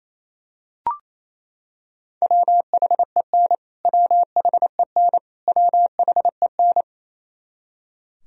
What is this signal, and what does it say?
Morse code, heard as a keyed tone, about 28 wpm, 700 Hz. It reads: WHEN WHEN WHEN
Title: Top 100 Words Repeated 28wpm